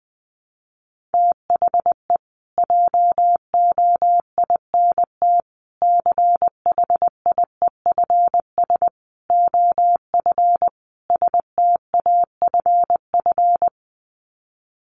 Morse code, heard as a keyed tone, about 20 wpm, 700 Hz. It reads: THE JOINT CHIEFS OF STAFF
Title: Sets of 5 Words Encore2 20wpm